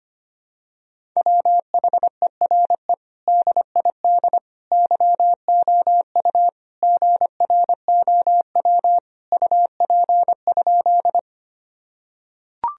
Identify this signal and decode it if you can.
Morse code, heard as a keyed tone, about 25 wpm, 700 Hz. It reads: WHERE DID YOU GROW UP?
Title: Sentences from Top 1000 Words - Review Part 1 - 25wpm